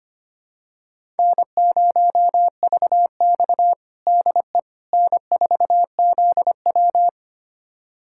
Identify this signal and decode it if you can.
Morse code, heard as a keyed tone, about 25 wpm, 700 Hz. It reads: N0VX DE N4ZW
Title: USA Callsign DE Callsign - 4 Characters - 25wpm